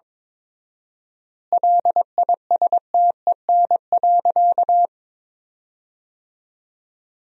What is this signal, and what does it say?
Morse code, heard as a keyed tone, about 22 wpm, 700 Hz. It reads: LISTEN.
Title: 4x Word-Spacing - Sentences from Top 500 Words - 22wpm